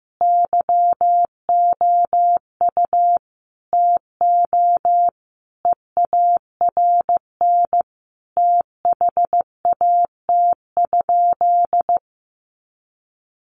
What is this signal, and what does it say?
Morse code, heard as a keyed tone, about 15 wpm, 700 Hz. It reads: YOU TO EARN THAT?
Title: Sentences from Top 1500 Words 15wpm